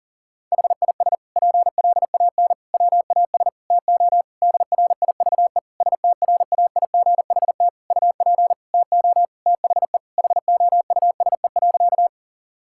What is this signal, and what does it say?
Morse code, heard as a keyed tone, about 37 wpm, 700 Hz. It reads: HIS PLAN WAS TO DRIVE STRAIGHT UP TO THE HOUSE.
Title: Sentences from Top 1000 Words - Review Part 6 - 40wpm